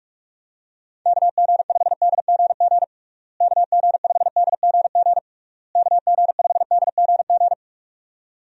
Morse code, heard as a keyed tone, about 46 wpm, 700 Hz. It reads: KG5DGG KG5DGG KG5DGG
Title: USA Call Signs - 6 Characters - Repeated 3x - 45wpm